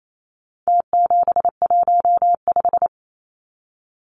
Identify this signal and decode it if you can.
Morse code, heard as a keyed tone, about 28 wpm, 700 Hz. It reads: T715
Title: Numbers - Contest Style - 4-Digit 28wpm